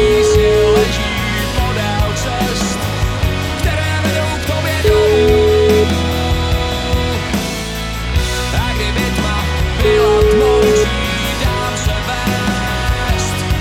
Ringback tone (Europe, 425 Hz), with noise.